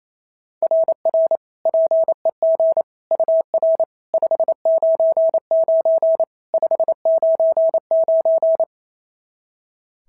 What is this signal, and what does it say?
Morse code, heard as a keyed tone, about 25 wpm, 650 Hz. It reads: RR PEG UR 599 599
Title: SOTA QSO Practice 28wpm